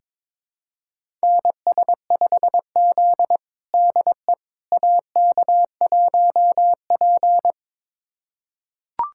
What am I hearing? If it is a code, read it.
Morse code, heard as a keyed tone, about 22 wpm, 700 Hz. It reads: NS5Z DE AK1P
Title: USA Callsign DE Callsign - 4 Characters - 22wpm